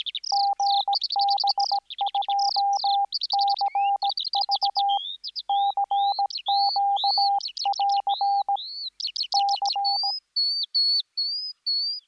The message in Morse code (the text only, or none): GB2CV CQ<AR> C